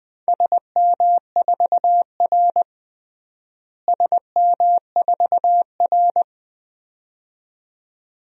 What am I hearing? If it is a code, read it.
Morse code, heard as a keyed tone, about 20 wpm, 700 Hz. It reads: SM4R SM4R
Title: DXCC-100 Call Signs - 4 Characters - Repeated 2x - 20wpm